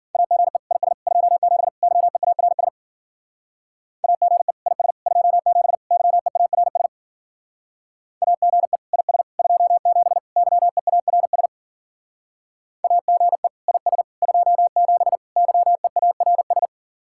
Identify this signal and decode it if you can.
Morse code, heard as a keyed tone, about 50 wpm, 700 Hz. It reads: AGE IS 27 YEARS AGE IS 27 YEARS AGE IS 27 YEARS AGE IS 27 YEARS